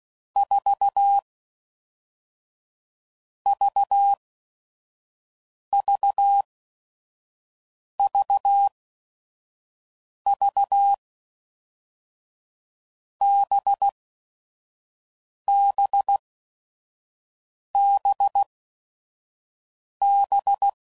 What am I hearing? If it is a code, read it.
Morse code, heard as a keyed tone, about 16 wpm, spaced as for 5 wpm, 800 Hz. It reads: 4 VVVV BBBB
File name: Groups / Basic Letter Groups / VB46